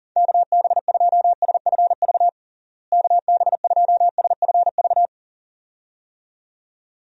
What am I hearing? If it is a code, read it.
Morse code, heard as a keyed tone, about 40 wpm, 700 Hz. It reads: KB2SFV KB2SFV
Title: USA Call Signs - 6 Characters - Repeated 2x - 40wpm